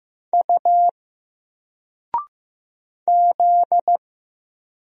Morse code, heard as a keyed tone, about 15 wpm, 700 Hz. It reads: U Z
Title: Single Letter-Number 15wpm